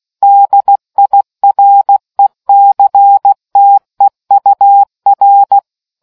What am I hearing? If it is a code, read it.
Morse code, heard as a keyed tone, about 16 wpm, 800 Hz. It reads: DIRECTEUR